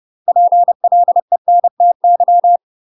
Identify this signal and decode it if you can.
Morse code, heard as a keyed tone, about 30 wpm, 700 Hz. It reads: PLENTY